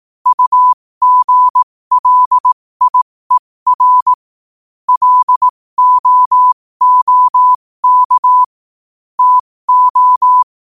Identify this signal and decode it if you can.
Morse code, heard as a keyed tone, about 18 wpm, spaced as for 15 wpm, 1000 Hz. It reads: UGLIER LOOK TO
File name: qotdfiles / 15wpm